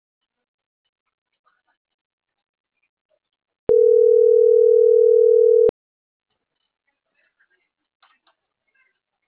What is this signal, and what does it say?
Ringback tone (North America), 440 + 480 Hz.